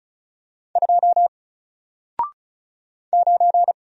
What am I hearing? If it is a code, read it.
Morse code, heard as a keyed tone, about 36 wpm, 700 Hz. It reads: 2 9
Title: Single Letter-Number Alt 35wpm